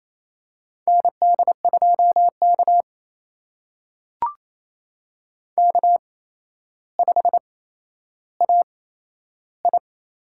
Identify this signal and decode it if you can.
Morse code, heard as a keyed tone, about 27 wpm, spaced as for 12 wpm, 700 Hz. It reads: ND2K K5AI
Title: USA Call Signs - 4 Characters - 8x Character Spacing - 28wpm